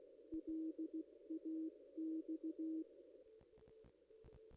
Morse code, heard as a keyed tone, about 16 wpm, 320 Hz. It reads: LAX
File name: contests / ss-cw-k5tr-op-n5rz -1600